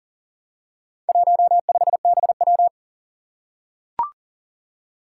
Morse code, heard as a keyed tone, about 40 wpm, 700 Hz. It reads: 15BW
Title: Four Challenging Letter-Number Combinations 40wpm